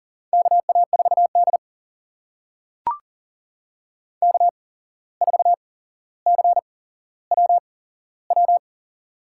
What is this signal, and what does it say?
Morse code, heard as a keyed tone, about 38 wpm, spaced as for 16 wpm, 700 Hz. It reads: KA4D K4CWW
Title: USA Call Signs - 8x Character Spacing - 40wpm